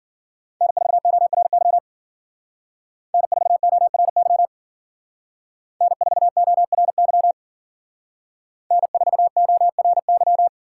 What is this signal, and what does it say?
Morse code, heard as a keyed tone, about 52 wpm, 700 Hz. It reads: N4ORY N4ORY N4ORY N4ORY